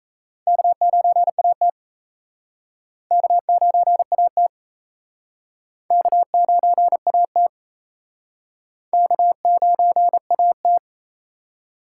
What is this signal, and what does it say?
Morse code, heard as a keyed tone, about 36 wpm, 700 Hz. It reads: K9AT K9AT K9AT K9AT